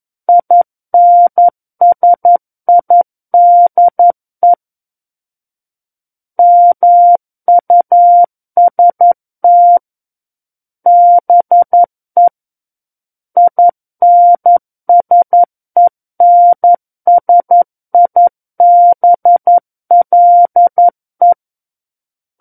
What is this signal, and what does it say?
Morse code, heard as a keyed tone, about 11 wpm, 700 Hz. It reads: INSIDE MUST BE INSENSIBLE
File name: 11 WPM